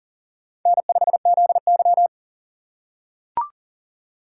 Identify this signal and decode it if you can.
Morse code, heard as a keyed tone, about 40 wpm, 700 Hz. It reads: N5ZY
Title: USA Call Signs - 4 Characters - 6x Character Spacing - 40wpm